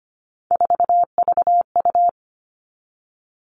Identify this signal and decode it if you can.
Morse code, heard as a keyed tone, about 25 wpm, 700 Hz. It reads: 4VU
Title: All Cut Numbers - 3-Digit 25wpm